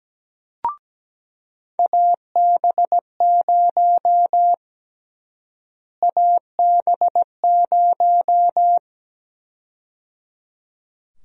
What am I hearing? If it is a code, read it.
Morse code, heard as a keyed tone, about 17 wpm, 700 Hz. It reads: AB0 AB0
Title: USA Call Sign Prefixes - Repeated 2x - 17wpm